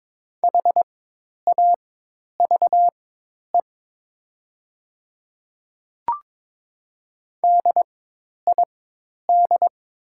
Morse code, heard as a keyed tone, about 22 wpm, spaced as for 10 wpm, 700 Hz. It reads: HAVE DID
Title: 4x Character Spacing - Top 100 Words - 22wpm